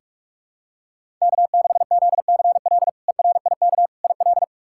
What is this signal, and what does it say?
Morse code, heard as a keyed tone, about 46 wpm, 700 Hz. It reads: K6ZCL ERIK IL
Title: K1USN SST QSO Practice - 45wpm